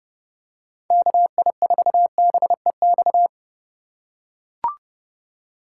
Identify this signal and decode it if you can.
Morse code, heard as a keyed tone, about 30 wpm, 700 Hz. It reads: KI4BEX
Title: USA Call Signs - 6 Characters - 3x Character Spacing - 30wpm